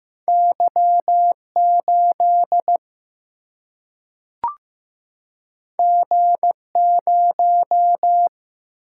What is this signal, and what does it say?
Morse code, heard as a keyed tone, about 15 wpm, 700 Hz. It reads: Y 8 G 0
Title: Single-Letter Single-Number 15wpm